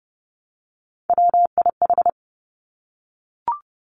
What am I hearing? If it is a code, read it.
Morse code, heard as a keyed tone, about 30 wpm, 700 Hz. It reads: WIH
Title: Three Challenging Letter-Number Combinations 30wpm